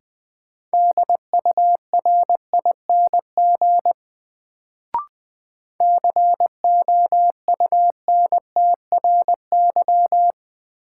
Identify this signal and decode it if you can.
Morse code, heard as a keyed tone, about 20 wpm, 700 Hz. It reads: DURING COUNTRY
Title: Sets of 2 Words - New Word Drill - 20wpm